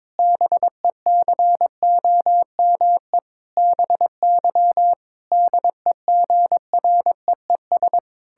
Morse code, heard as a keyed tone, about 22 wpm, 700 Hz. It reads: BECOME BY DEGREES
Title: Pangram Challenge v2 - 22wpm